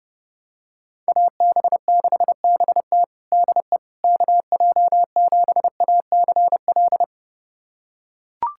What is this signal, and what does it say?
Morse code, heard as a keyed tone, about 32 wpm, 700 Hz. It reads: AB6BT DE KJ7ACL
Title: USA Callsign DE Callsign 30wpm